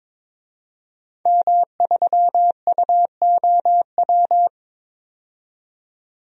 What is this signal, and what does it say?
Morse code, heard as a keyed tone, about 22 wpm, 700 Hz. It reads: M3UOW
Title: DXCC-100 Call Signs - 5 Characters - 22wpm